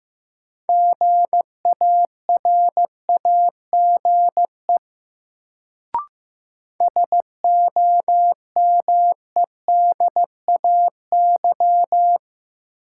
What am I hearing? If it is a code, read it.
Morse code, heard as a keyed tone, about 16 wpm, 700 Hz. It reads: GARAGE SOMEDAY
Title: Sentences from Top 1600 Words - New Word Drill Alt - 15wpm